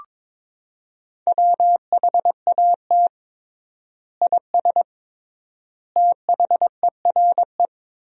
Morse code, heard as a keyed tone, about 22 wpm, 700 Hz. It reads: WHAT IS THERE
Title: 3x Word-Spacing - Sentences from Top 100 Words - 22wpm